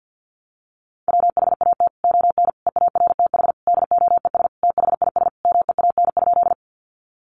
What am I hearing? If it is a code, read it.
Morse code, heard as a keyed tone, about 50 wpm, 700 Hz. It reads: WHAT ON EARTH DOES THIS MEAN?
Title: Sentences from Top 200 Words 50wpm